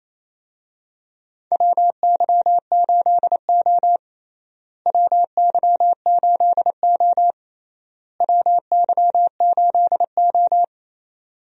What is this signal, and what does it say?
Morse code, heard as a keyed tone, about 28 wpm, 700 Hz. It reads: WY8O WY8O WY8O
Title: USA Call Signs - 4 Characters - Repeated 3x - 28wpm